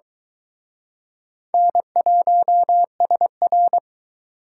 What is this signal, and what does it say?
Morse code, heard as a keyed tone, about 24 wpm, 700 Hz. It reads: N1SR